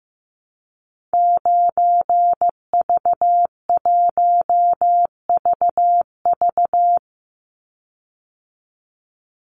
Morse code, heard as a keyed tone, about 15 wpm, 700 Hz. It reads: 9V1VV
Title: Call Signs Repeated 2x - 15wpm